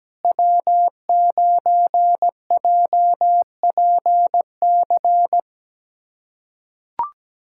Morse code, heard as a keyed tone, about 17 wpm, 700 Hz. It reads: W9JPC
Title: USA Callsign DE Callsign - 5 Characters - 17wpm